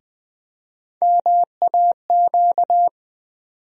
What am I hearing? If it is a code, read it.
Morse code, heard as a keyed tone, about 21 wpm, 700 Hz. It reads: MAQ
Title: USA Call Sign Postfixes - 3 Letters 20wpm